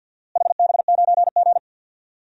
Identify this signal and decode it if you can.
Morse code, heard as a keyed tone, about 50 wpm, 700 Hz. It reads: HB9G
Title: DXCC-100 Call Signs - 4 Characters - Repeated 2x - 50wpm